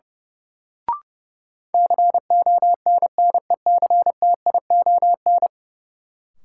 Morse code, heard as a keyed tone, about 30 wpm, 700 Hz. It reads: CONNECTION
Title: Sentences from Top 1300 Words - New Word Drill - 30wpm